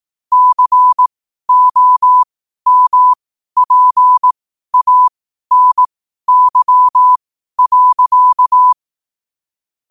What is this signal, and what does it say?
Morse code, heard as a keyed tone, about 18 wpm, spaced as for 12 wpm, 1000 Hz. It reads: COMPANY.
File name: qotdfiles / 12wpm